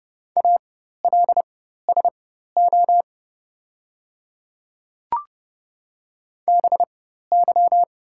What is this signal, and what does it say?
Morse code, heard as a keyed tone, about 29 wpm, spaced as for 14 wpm, 700 Hz. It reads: ALSO BY
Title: 4x Character Spacing - Top 300 Words - 30wpm